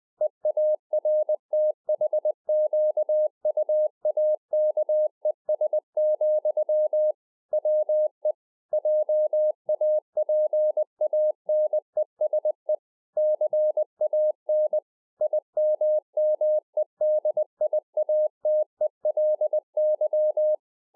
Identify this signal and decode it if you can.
Morse code, heard as a keyed tone, about 20 wpm, 610 Hz. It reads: EARTHQUAKES, WE JAPANESE CAN IMMEDIATELY